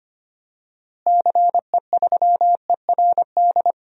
Morse code, heard as a keyed tone, about 25 wpm, 700 Hz. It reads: CE3ERD
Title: DXCC-100 Call Signs - 6 Characters - Repeated 2x - 25wpm